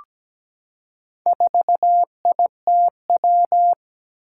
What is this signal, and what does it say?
Morse code, heard as a keyed tone, about 17 wpm, 700 Hz. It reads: 4ITW
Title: Four Challenging Letter-Number Combinations 17wpm